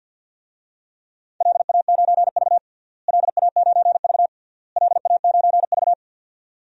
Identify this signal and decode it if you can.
Morse code, heard as a keyed tone, about 51 wpm, 700 Hz. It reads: LA9V LA9V LA9V
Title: DXCC-100 Call Signs - 4 Characters - Repeated 3x - 50wpm